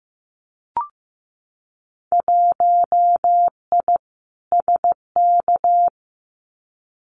Morse code, heard as a keyed tone, about 15 wpm, 700 Hz. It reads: 1I SK
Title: Winter Field Day Exchanges Encore 15wpm